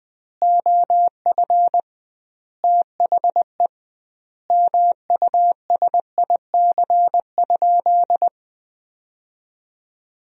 Morse code, heard as a keyed tone, about 20 wpm, 700 Hz. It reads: OF THE MUSIC?
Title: 2x Word-Spacing - Sentences from Top 300 Words - 20wpm